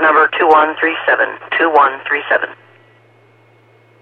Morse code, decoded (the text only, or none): none